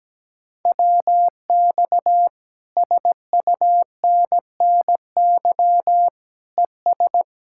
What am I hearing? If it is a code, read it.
Morse code, heard as a keyed tone, about 17 wpm, 700 Hz. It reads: WX SUNNY ES
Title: Speed-Racing Common QSO Phrases - 17wpm